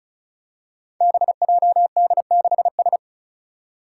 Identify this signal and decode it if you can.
Morse code, heard as a keyed tone, about 36 wpm, 700 Hz. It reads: BJD6S
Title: Five Challenging Letter-Number Combinations 35wpm